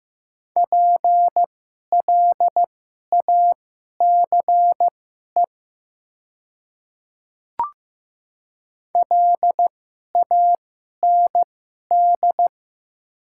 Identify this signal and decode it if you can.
Morse code, heard as a keyed tone, about 15 wpm, spaced as for 11 wpm, 700 Hz. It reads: PLACE LAND
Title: 2x Character Spacing - Top 200 Words - 15wpm